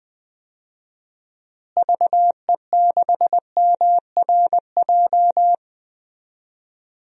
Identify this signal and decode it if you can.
Morse code, heard as a keyed tone, about 21 wpm, 700 Hz. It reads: VE6MRJ
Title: DXCC-100 Call Signs - 6 Characters - Repeated 2x - 20wpm